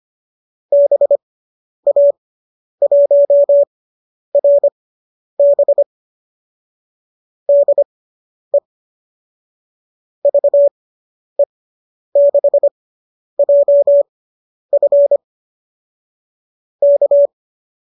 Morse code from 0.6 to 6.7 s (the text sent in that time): BA1RB